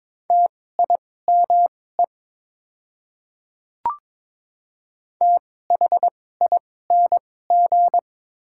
Morse code, heard as a keyed tone, about 23 wpm, spaced as for 16 wpm, 700 Hz. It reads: TIME THING
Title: 2x Character Spacing - Top 200 Words - 22wpm